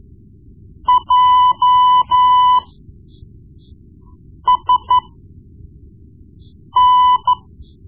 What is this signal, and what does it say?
Morse code, heard as a keyed tone, about 13 wpm, spaced as for 4 wpm, 1000 Hz. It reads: JSN
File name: Morse Lessons / Morse Lessons / Tape-1B